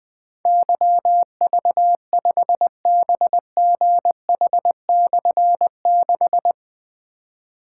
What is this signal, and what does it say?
Morse code, heard as a keyed tone, about 21 wpm, 700 Hz. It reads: YV5BGH/6
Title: Difficult Call Signs 20wpm